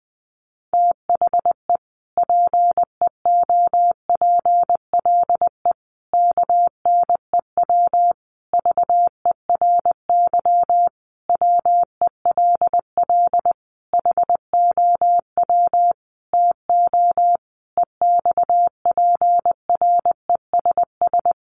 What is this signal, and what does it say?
Morse code, heard as a keyed tone, about 20 wpm, 700 Hz. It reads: THE PEOPLE KNEW VERY WELL HOW TO EXPRESS
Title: Sentences from Top 1100 Words 20wpm